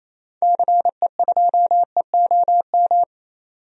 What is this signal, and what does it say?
Morse code, heard as a keyed tone, about 28 wpm, 700 Hz. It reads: CE2EOM
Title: DXCC-100 Call Signs - 6 Characters - 28wpm